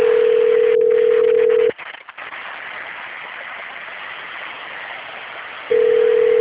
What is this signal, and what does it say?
Ringback tone (North America), 440 + 480 Hz, with noise.
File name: out- - - - - .54143